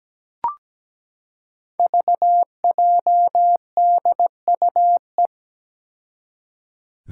Morse code, heard as a keyed tone, about 17 wpm, 700 Hz. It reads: VJDUE
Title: Five Challenging Letter-Number Combinations 17wpm